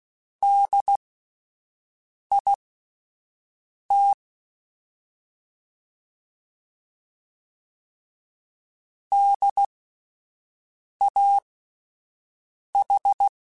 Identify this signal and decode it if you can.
Morse code, heard as a keyed tone, about 16 wpm, spaced as for 5 wpm, 800 Hz. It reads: DIT DAH